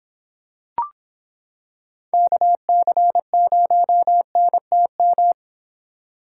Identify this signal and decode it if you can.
Morse code, heard as a keyed tone, about 26 wpm, 700 Hz. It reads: KC0NTM